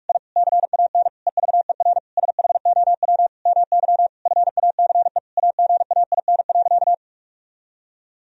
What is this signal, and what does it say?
Morse code, heard as a keyed tone, about 48 wpm, 700 Hz. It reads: I CAN EVER SHOW MY FACE AGAIN.
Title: Sentences from Top 1000 Words - Review Part 2 - 45wpm